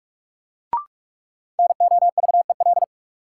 Morse code, heard as a keyed tone, about 42 wpm, 700 Hz. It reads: NOVEL
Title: Sentences from Top 1300 Words - New Word Drill Alt - 45wpm